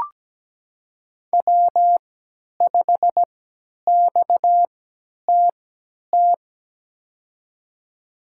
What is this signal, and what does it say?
Morse code, heard as a keyed tone, about 17 wpm, spaced as for 10 wpm, 700 Hz. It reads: W5XTT
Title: USA Call Signs - 3x Character Spacing - 17wpm